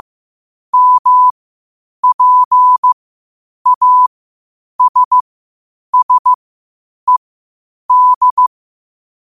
Morse code, heard as a keyed tone, about 15 wpm, spaced as for 8 wpm, 1000 Hz. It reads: MPASSED
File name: qotdfiles / 8wpm